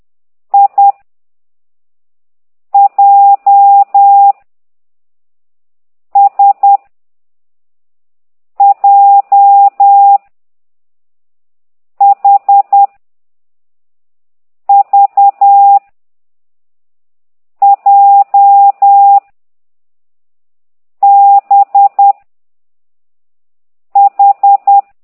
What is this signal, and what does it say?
Morse code, heard as a keyed tone, about 10 wpm, 800 Hz. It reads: I J S J H V J B H
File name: exerc7